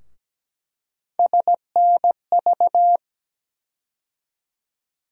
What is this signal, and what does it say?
Morse code, heard as a keyed tone, about 17 wpm, 700 Hz. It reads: SNV
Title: USA Call Sign Postfixes - 3 Letters 17wpm